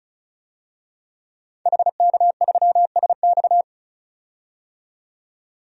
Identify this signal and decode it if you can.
Morse code, heard as a keyed tone, about 34 wpm, 700 Hz. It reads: HK3SX
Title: DXCC-100 Call Signs - 5 Characters - 35wpm